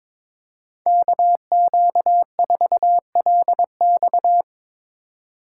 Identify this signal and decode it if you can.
Morse code, heard as a keyed tone, about 22 wpm, 700 Hz. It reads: KQ4LX
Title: USA Call Signs - 5 Characters - 3x Character Spacing - 22wpm